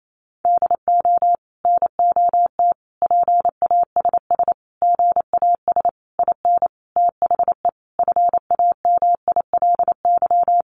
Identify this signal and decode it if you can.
Morse code, heard as a keyed tone, about 28 wpm, 700 Hz. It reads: DO NOT PASS GAS IN THE FAMILY
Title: Sentences from Top 1100 Words 28wpm